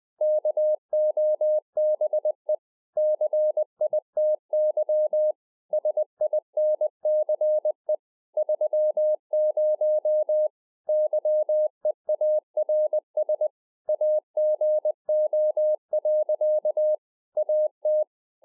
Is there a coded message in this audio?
Yes: Morse code from 0.2 to 18.0 s, heard as a keyed tone, about 20 wpm, 610 Hz. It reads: KOBE CITY SINCE 30 YEARS AGO. AT